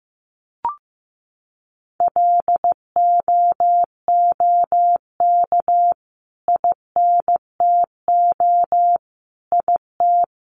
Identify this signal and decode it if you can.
Morse code, heard as a keyed tone, about 14 wpm, 700 Hz. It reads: LOOK INTO IT
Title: Sentences from Top 1400 Words 15wpm